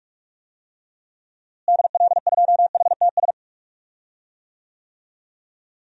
Morse code, heard as a keyed tone, about 46 wpm, 700 Hz. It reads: DL2HTS
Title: DXCC-100 Call Signs - 6 Characters - Repeated 3x - 45wpm